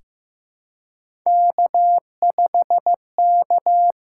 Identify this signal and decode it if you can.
Morse code, heard as a keyed tone, about 15 wpm, 700 Hz. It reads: K5K
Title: USA Call Signs - 5 Characters - 6x Character Spacing - 15wpm